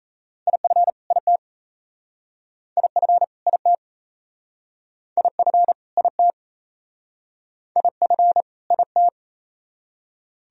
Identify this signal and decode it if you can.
Morse code, heard as a keyed tone, about 35 wpm, 700 Hz. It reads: IF IT IF IT IF IT IF IT
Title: Speed-Racing - Sets of 2 Words 28wpm